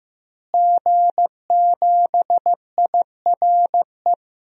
Morse code, heard as a keyed tone, about 16 wpm, 700 Hz. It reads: G7IRE
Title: DXCC-100 Call Signs - 5 Characters - Repeated 2x - 15wpm